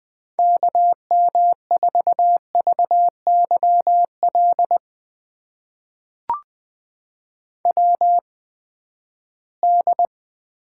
Morse code, heard as a keyed tone, about 20 wpm, spaced as for 14 wpm, 700 Hz. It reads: KM4VYL WD
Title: USA Call Signs - 6 Characters - 8x Character Spacing - 20wpm